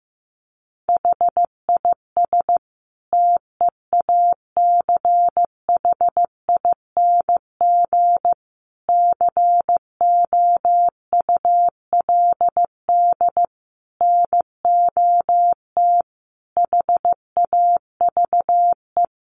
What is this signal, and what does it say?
Morse code, heard as a keyed tone, about 15 wpm, 700 Hz. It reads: HIS TEACHING COULD NOT HAVE
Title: Sentences from Top 1100 Words 15wpm